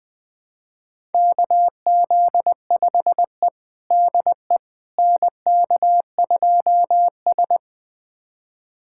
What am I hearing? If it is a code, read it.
Morse code, heard as a keyed tone, about 21 wpm, 700 Hz. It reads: KZ5E DE NK2S
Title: USA Callsign DE Callsign - 4 Characters - 20wpm